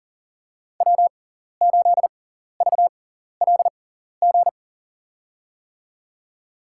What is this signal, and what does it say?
Morse code, heard as a keyed tone, about 37 wpm, spaced as for 14 wpm, 700 Hz. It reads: W8VLG
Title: USA Call Signs - 5 Characters - 6x Character Spacing - 40wpm